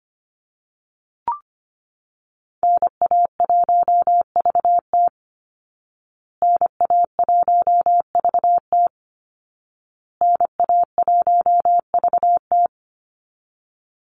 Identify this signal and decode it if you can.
Morse code, heard as a keyed tone, about 25 wpm, 700 Hz. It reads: NA1VT NA1VT NA1VT
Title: Call Signs Repeated 2x - 25wpm